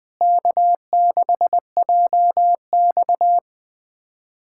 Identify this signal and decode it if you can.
Morse code, heard as a keyed tone, about 21 wpm, 700 Hz. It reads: K6JX
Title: USA Callsign DE Callsign 20wpm